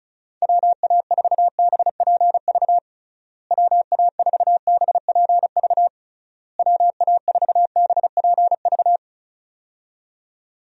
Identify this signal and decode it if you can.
Morse code, heard as a keyed tone, about 34 wpm, 700 Hz. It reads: WA4BPV WA4BPV WA4BPV
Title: USA Call Signs - 6 Characters - Repeated 3x - 35wpm